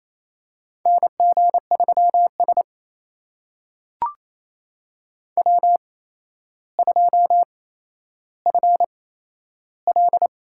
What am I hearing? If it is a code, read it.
Morse code, heard as a keyed tone, about 28 wpm, spaced as for 12 wpm, 700 Hz. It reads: NG3S W2FL
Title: USA Call Signs - 4 Characters - 8x Character Spacing - 28wpm